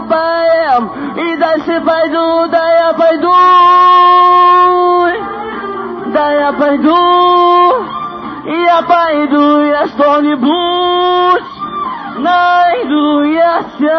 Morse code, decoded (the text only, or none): none